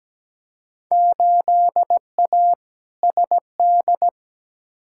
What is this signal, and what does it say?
Morse code, heard as a keyed tone, about 17 wpm, 700 Hz. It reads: 8A SD
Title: Field Day Exchanges Encore - 17wpm